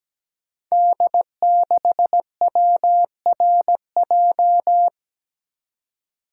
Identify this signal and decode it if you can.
Morse code, heard as a keyed tone, about 17 wpm, 700 Hz. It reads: D6WRJ
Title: Code Groups - 5 Letter:Numbers - 17wpm